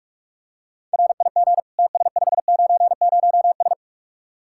Morse code, heard as a keyed tone, about 44 wpm, 700 Hz. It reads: RIG TS590S